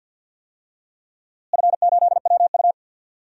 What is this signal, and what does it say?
Morse code, heard as a keyed tone, about 52 wpm, 700 Hz. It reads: F8WU
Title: DXCC-100 Call Signs - 4 Characters - Repeated 3x - 50wpm